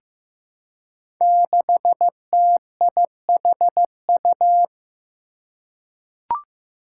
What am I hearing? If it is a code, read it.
Morse code, heard as a keyed tone, about 15 wpm, 700 Hz. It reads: 6TIHU
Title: Five Challenging Letter-Number Combinations 15wpm